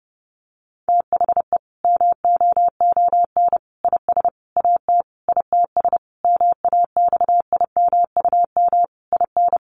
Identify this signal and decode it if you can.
Morse code, heard as a keyed tone, about 30 wpm, 700 Hz. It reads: THE MOON IS AT ITS MAXIMUM IN
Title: Sentences from Top 1300 Words 30wpm